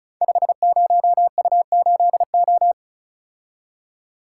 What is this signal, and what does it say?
Morse code, heard as a keyed tone, about 36 wpm, 700 Hz. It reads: 50U8O
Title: Code Groups - 5 Letter:Numbers - 35wpm